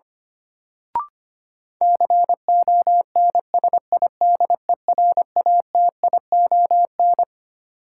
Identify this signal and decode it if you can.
Morse code, heard as a keyed tone, about 25 wpm, 700 Hz. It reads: CONSIDERATION
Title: Sentences from Top 1500 Words - New Word Drill - 25wpm